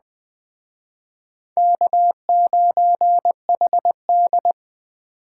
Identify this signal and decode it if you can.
Morse code, heard as a keyed tone, about 21 wpm, 700 Hz. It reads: K9HD